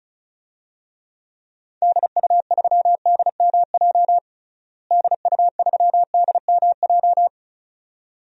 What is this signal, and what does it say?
Morse code, heard as a keyed tone, about 36 wpm, 700 Hz. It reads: DU3DMJ DU3DMJ
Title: DXCC-100 Call Signs - 6 Characters - Repeated 2x - 35wpm